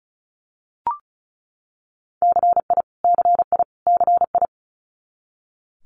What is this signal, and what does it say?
Morse code, heard as a keyed tone, about 35 wpm, 700 Hz. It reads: CI CI CI
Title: Multi-letter Phonemes Repeated 3x 35wpm